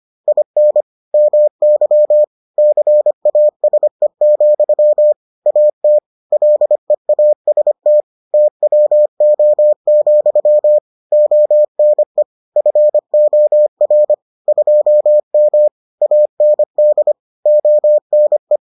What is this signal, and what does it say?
Morse code, heard as a keyed tone, about 25 wpm, 600 Hz. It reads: IN MY CASE, AT LEAST TWO, ONE FOR 2M AND ONE